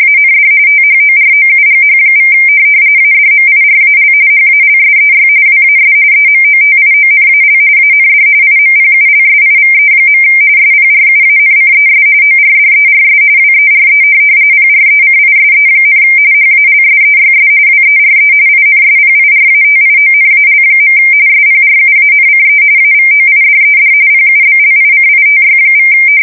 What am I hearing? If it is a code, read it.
RTTY, about 45.45 baud. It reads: AND THOROUGH= ADMIRAL HALSEY WAS EMOTIONAL AND IMPULSIVE. SPRUANCES ORDERS WERE PRECISE, HALSEYS WERE INSPIRATIONAL. SPRUANCE AVOIDED PUBLICITY=
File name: fsk170 / USS-New-Jersey